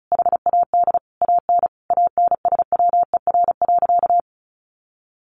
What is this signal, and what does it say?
Morse code, heard as a keyed tone, about 34 wpm, 700 Hz. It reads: HAD AN ANSWER.
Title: Sentences from Top 1200 Words 35wpm